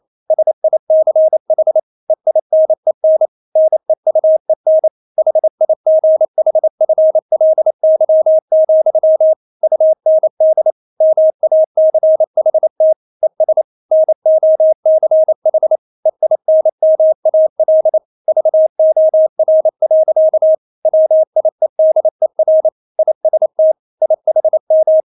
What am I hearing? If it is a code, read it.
Morse code, heard as a keyed tone, about 28 wpm, 630 Hz. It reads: SICH EINEN NEUEN HIGHFLY, UND MACHT ES NOCH EINMAL VOR. WIEDER IST IHM
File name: Witz9 28wpm